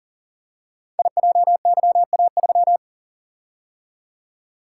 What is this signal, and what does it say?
Morse code, heard as a keyed tone, about 42 wpm, 700 Hz. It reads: IJYA3
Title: Code Groups - 5 Letter:Numbers - 40wpm